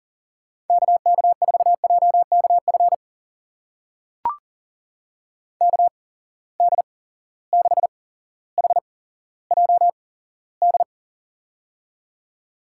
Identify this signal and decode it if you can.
Morse code, heard as a keyed tone, about 40 wpm, spaced as for 17 wpm, 700 Hz. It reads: KK4JKF KD6HJD
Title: USA Call Signs - 6 Characters - 8x Character Spacing - 40wpm